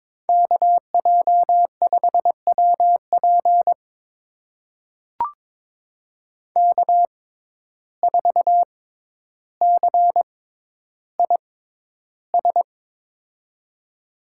Morse code, heard as a keyed tone, about 22 wpm, spaced as for 11 wpm, 700 Hz. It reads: KJ5WP K4CIS
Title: USA Call Signs - 6x Character Spacing - 22wpm